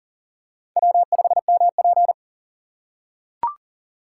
Morse code, heard as a keyed tone, about 42 wpm, 700 Hz. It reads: W5MP
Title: USA Call Signs - 4 Characters - 6x Character Spacing - 40wpm